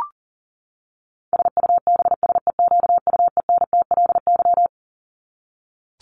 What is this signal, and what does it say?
Morse code, heard as a keyed tone, about 37 wpm, 700 Hz. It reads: SUBSEQUENTLY
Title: LY Suffix - Top 100 Words with Base Words 40wpm